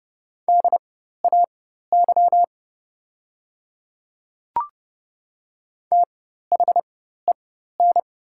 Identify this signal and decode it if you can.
Morse code, heard as a keyed tone, about 30 wpm, spaced as for 14 wpm, 700 Hz. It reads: DAY THEN
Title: 4x Character Spacing - Top 100 Words - 30wpm